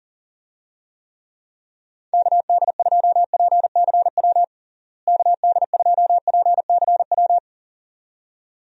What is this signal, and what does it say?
Morse code, heard as a keyed tone, about 40 wpm, 700 Hz. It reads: KD2PCW KD2PCW
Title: USA Call Signs - 6 Characters - Repeated 2x - 40wpm